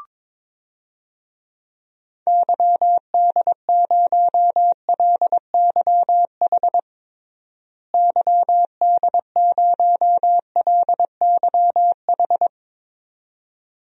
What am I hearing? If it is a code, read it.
Morse code, heard as a keyed tone, about 22 wpm, 700 Hz. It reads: YD0LYH YD0LYH
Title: DXCC-100 Call Signs - 6 Characters - Repeated 2x - 22wpm